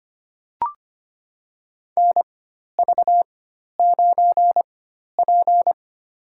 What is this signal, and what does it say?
Morse code, heard as a keyed tone, about 25 wpm, spaced as for 12 wpm, 700 Hz. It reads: NV9P
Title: USA Call Signs - 4x Character Spacing - 25wpm